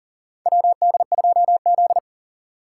Morse code, heard as a keyed tone, about 40 wpm, 700 Hz. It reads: WD2Z
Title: USA Call Signs - 4 Characters - 2x Character Spacing - 40wpm